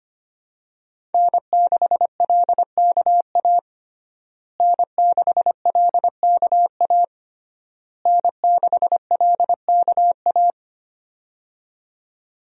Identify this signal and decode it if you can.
Morse code, heard as a keyed tone, about 25 wpm, 700 Hz. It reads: N6LKA N6LKA N6LKA
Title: USA Call Signs - 5 Characters - Repeated 3x - 25wpm